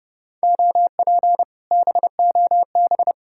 Morse code, heard as a keyed tone, about 40 wpm, spaced as for 35 wpm, 700 Hz. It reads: OP BOB
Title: Speed-Racing Common QSO Phrases - 30wpm